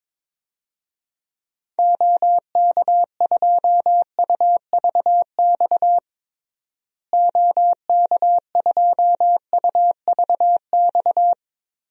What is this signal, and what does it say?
Morse code, heard as a keyed tone, about 22 wpm, 700 Hz. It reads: OK2UVX OK2UVX
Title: DXCC-100 Call Signs - 6 Characters - Repeated 3x - 22wpm